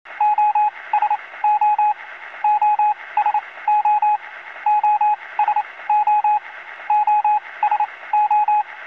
Morse code, heard as a keyed tone, about 27 wpm, spaced as for 20 wpm, 840 Hz, with radio noise. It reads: OSO OSO OSO OSO